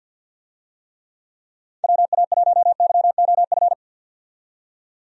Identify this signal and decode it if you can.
Morse code, heard as a keyed tone, about 53 wpm, 700 Hz. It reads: WA1YOF
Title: USA Call Signs - 6 Characters - 50wpm